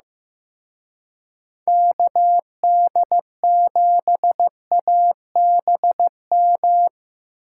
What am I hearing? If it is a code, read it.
Morse code, heard as a keyed tone, about 17 wpm, 700 Hz. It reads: KD7ABM
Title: Speed Racing - USA 2x3 Call Signs 15wpm